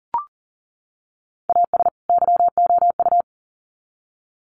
Morse code, heard as a keyed tone, about 40 wpm, 700 Hz. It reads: AS YOU
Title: Sets of 2 Words 40wpm